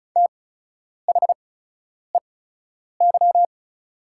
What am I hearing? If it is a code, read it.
Morse code, heard as a keyed tone, about 36 wpm, spaced as for 10 wpm, 700 Hz. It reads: THEY
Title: 8x Character Spacing - Top 200 Words - 35wpm